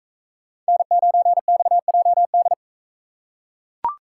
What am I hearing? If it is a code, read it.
Morse code, heard as a keyed tone, about 38 wpm, 700 Hz. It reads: N9XJD